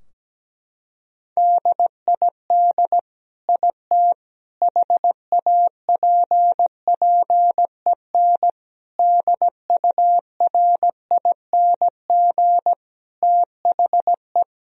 Morse code, heard as a keyed tone, about 17 wpm, 700 Hz. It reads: DID IT HAPPEN DURING THE
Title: Sentences from Top 1000 Words - Review Part 2 - 17wpm